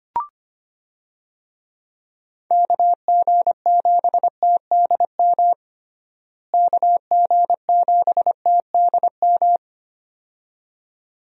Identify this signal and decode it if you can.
Morse code, heard as a keyed tone, about 25 wpm, 700 Hz. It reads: KG7TDM KG7TDM
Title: USA Call Signs - 6 Characters - Repeated 2x - 25wpm